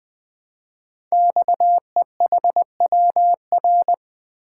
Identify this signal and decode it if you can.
Morse code, heard as a keyed tone, about 20 wpm, 700 Hz. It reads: XEHWR
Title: Code Groups - 5 Letters - 20wpm